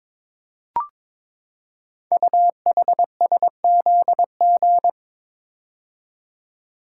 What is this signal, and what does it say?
Morse code, heard as a keyed tone, about 22 wpm, 700 Hz. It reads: UHSZG
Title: Code Groups - 5 Letters - 22wpm